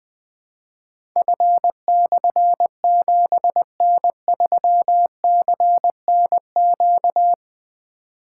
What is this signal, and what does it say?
Morse code, heard as a keyed tone, about 20 wpm, 700 Hz. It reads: F/7N3CNQ
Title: Difficult Call Signs 20wpm